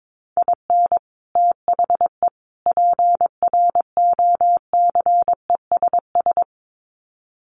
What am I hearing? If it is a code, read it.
Morse code, heard as a keyed tone, about 22 wpm, 700 Hz. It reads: IN THE PROCESS
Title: Sets of 3 Words 22wpm